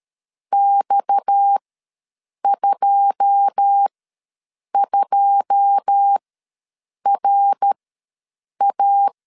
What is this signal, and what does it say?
Morse code, heard as a keyed tone, about 13 wpm, spaced as for 7 wpm, 800 Hz. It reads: X22RA
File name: LEZIONI / 9 2 ws 12wpm 3406